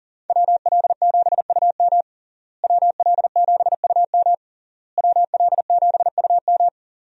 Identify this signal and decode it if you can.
Morse code, heard as a keyed tone, about 37 wpm, 700 Hz. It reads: WL7UM WL7UM WL7UM
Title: USA Call Signs - 5 Characters - Repeated 3x - 40wpm